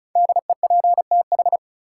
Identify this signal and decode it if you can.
Morse code, heard as a keyed tone, about 36 wpm, 700 Hz. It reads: DEPTH
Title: Sentences from Top 1200 Words - New Word Drill Alt - 35wpm